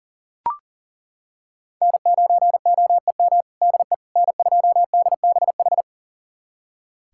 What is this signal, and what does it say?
Morse code, heard as a keyed tone, about 40 wpm, 700 Hz. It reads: N9OEM DE N2DBH
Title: USA Callsign DE Callsign - 5 Characters - 40wpm